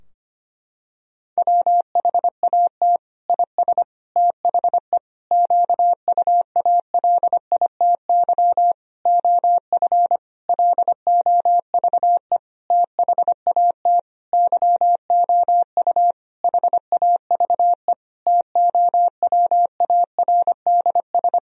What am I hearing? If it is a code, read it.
Morse code, heard as a keyed tone, about 25 wpm, 700 Hz. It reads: WHAT IS THE QUALITY OF LOVE THAT YOU HAVE TOWARDS
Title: Sentences from Top 1000 Words - Review Part 4 - 25wpm